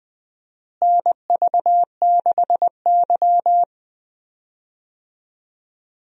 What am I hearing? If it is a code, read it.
Morse code, heard as a keyed tone, about 20 wpm, 700 Hz. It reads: NV6Y
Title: USA Call Signs - 4 Characters - Repeated 3x - 20wpm